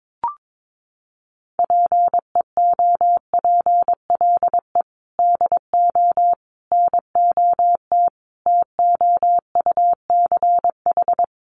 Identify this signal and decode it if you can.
Morse code, heard as a keyed tone, about 22 wpm, 700 Hz. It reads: PEOPLE DO NOT TOUCH
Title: Sentences from Top 1200 Words 22wpm